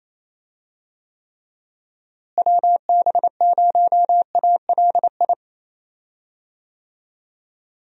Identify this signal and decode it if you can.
Morse code, heard as a keyed tone, about 27 wpm, 700 Hz. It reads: WB0ALI
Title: USA Call Signs - 6 Characters - Repeated 2x - 28wpm